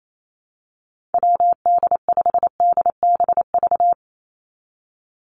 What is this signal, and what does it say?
Morse code, heard as a keyed tone, about 28 wpm, 700 Hz. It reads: WD5DBV
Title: Call Signs Repeated 2x - 28wpm